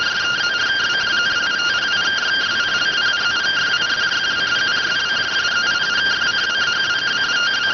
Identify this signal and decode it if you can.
Two-tone teleprinter signal, about 170 Hz shift.